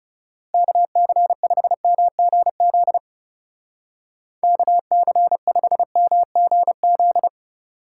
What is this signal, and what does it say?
Morse code, heard as a keyed tone, about 37 wpm, 700 Hz. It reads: KC5MGZ KC5MGZ
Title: Speed Racing - USA 2x3 Call Signs 30wpm